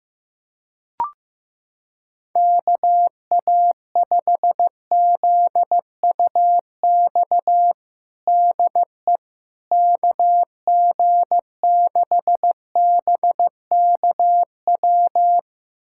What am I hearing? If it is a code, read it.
Morse code, heard as a keyed tone, about 15 wpm, 700 Hz. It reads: KA5ZUX DE KG6BKW
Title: USA Callsign DE Callsign 15wpm